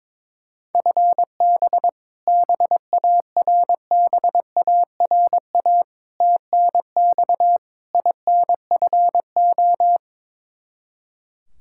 Morse code, heard as a keyed tone, about 22 wpm, 700 Hz. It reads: FB BARBARA TNX INFO
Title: Practice QSO - Exchange 3 - More Info - 22wpm